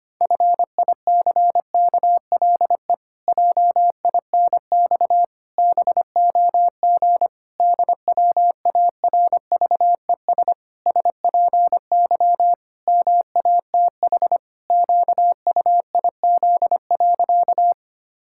Morse code, heard as a keyed tone, about 25 wpm, 700 Hz. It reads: FICKLE JINX BOG DWARVES SPY MATH QUIZ.
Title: Pangram Challenge v2 - 25wpm